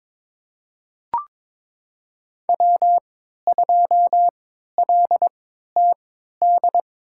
Morse code, heard as a keyed tone, about 22 wpm, spaced as for 13 wpm, 700 Hz. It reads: W2LTD
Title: USA Call Signs - 5 Characters - 3x Character Spacing - 22wpm